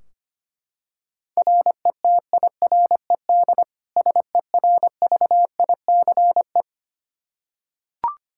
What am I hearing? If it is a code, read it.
Morse code, heard as a keyed tone, about 25 wpm, 700 Hz. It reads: RETIRED SERVICE
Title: K1USN SST QSO Practice 25wpm